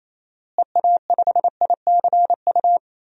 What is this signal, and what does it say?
Morse code, heard as a keyed tone, about 28 wpm, 700 Hz. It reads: EA5ICU
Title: DXCC-100 Call Signs - 6 Characters - Repeated 3x - 28wpm